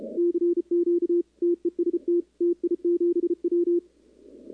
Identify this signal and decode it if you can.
Morse code, heard as a keyed tone, about 34 wpm, 340 Hz. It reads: CQ TEST TI7W
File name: contests / 2019-cw-wpx-kt5j-k5pi- -2330 0-11